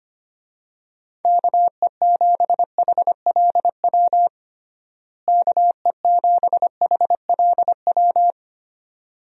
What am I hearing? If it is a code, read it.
Morse code, heard as a keyed tone, about 25 wpm, 700 Hz. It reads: KE7HLW KE7HLW
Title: USA Call Signs - 6 Characters - Repeated 2x - 25wpm